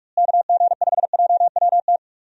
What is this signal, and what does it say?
Morse code, heard as a keyed tone, about 46 wpm, 700 Hz. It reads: KG5JWT
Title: USA Call Signs - 6 Characters - Repeated 3x - 45wpm